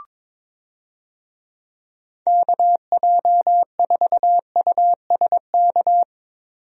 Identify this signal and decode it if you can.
Morse code, heard as a keyed tone, about 22 wpm, 700 Hz. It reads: KJ4USK
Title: USA Call Signs - 6 Characters - 22wpm